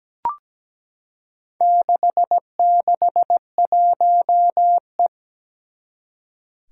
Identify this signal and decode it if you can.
Morse code, heard as a keyed tone, about 17 wpm, 700 Hz. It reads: 661E
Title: Four Challenging Letter-Number Combinations 17wpm